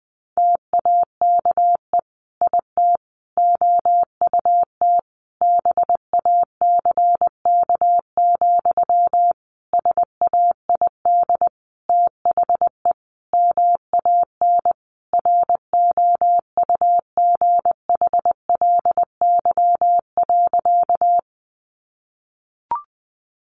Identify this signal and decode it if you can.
Morse code, heard as a keyed tone, about 20 wpm, 700 Hz. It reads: TAKE IT OUT BACK, SAID THE MAN ROUGHLY.
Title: Sentences from Top 1500 Words 20wpm